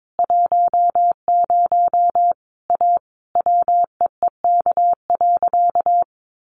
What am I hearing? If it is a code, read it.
Morse code, heard as a keyed tone, about 22 wpm, 700 Hz. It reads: 10 A WEEK.
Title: Easy Phrases 22wpm